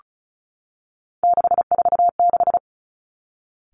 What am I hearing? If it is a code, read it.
Morse code, heard as a keyed tone, about 36 wpm, 700 Hz. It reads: 646
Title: Three Challenging Letter-Number Combinations 35wpm